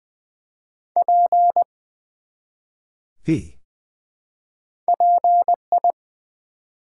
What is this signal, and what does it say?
Morse code, heard as a keyed tone, about 20 wpm, 700 Hz. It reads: P PI
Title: Build A Word - 1 Letter At A Time - 3 to 5-Letter Words - 20wpm